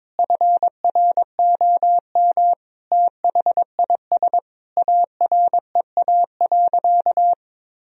Morse code, heard as a keyed tone, about 22 wpm, 700 Hz. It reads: FROM THIS AREA.
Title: Sentences from Top 1000 Words - Review Part 2 - 22wpm